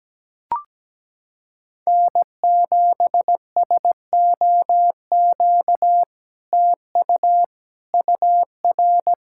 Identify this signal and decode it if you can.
Morse code, heard as a keyed tone, about 17 wpm, 700 Hz. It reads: N7SOQ TU UR
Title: SOTA QSO Practice 17wpm